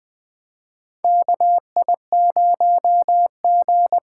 Morse code, heard as a keyed tone, about 20 wpm, 700 Hz. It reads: KI0G
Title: 2023 SOTA Chaser Call Signs - 20wpm